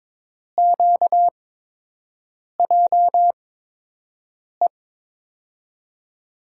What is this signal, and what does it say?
Morse code, heard as a keyed tone, about 22 wpm, spaced as for 6 wpm, 700 Hz. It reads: QJE
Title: USA Call Signs - 6 Characters - 8x Character Spacing - 22wpm